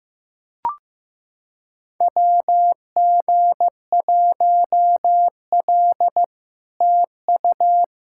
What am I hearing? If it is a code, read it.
Morse code, heard as a keyed tone, about 15 wpm, 700 Hz. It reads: WG1L TU
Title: SOTA QSO Practice 15wpm